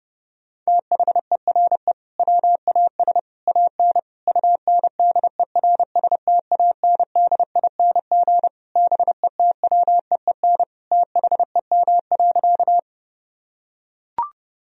Morse code, heard as a keyed tone, about 30 wpm, 700 Hz. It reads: THERE WAS AN UNDERSTANDING BETWEEN THEM.
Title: Sentences from Top 1000 Words - Review Part 3 - 30wpm